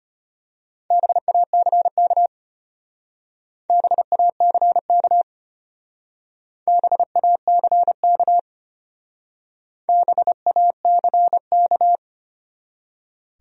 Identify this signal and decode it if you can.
Morse code, heard as a keyed tone, about 31 wpm, 700 Hz. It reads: BACK BACK BACK BACK
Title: Speed-Racing Top 200 Words 25wpm